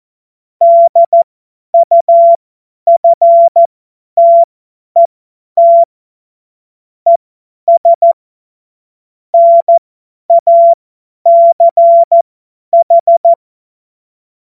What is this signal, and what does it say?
Morse code, heard as a keyed tone, about 14 wpm, spaced as for 10 wpm, 680 Hz. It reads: DUFTET ES NACH
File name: Morsekurs / Witze / Witz 16. Februar 2021, 50 F70